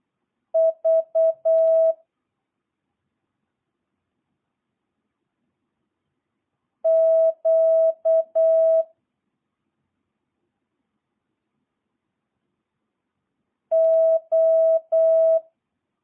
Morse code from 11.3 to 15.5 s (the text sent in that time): O